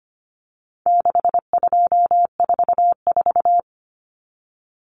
Morse code, heard as a keyed tone, about 25 wpm, 700 Hz. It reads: 6244
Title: Numbers - 4-Digit 25wpm